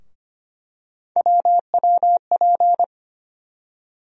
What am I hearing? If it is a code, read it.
Morse code, heard as a keyed tone, about 25 wpm, 700 Hz. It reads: WWP
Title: USA Call Sign Postfixes - 3 Letters 25wpm